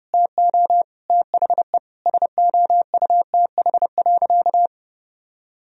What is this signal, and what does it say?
Morse code, heard as a keyed tone, about 30 wpm, 700 Hz. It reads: TO THE SOUTH.
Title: Sentences from Top 400 Words and Abbreviations 30wpm